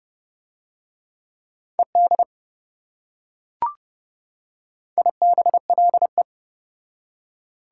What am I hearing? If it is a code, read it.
Morse code, heard as a keyed tone, about 28 wpm, 700 Hz. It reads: ED IBLE